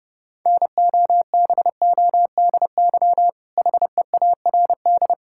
Morse code, heard as a keyed tone, about 30 wpm, 700 Hz. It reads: NOBODY HEARD
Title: Sentences from Top 1600 Words 30wpm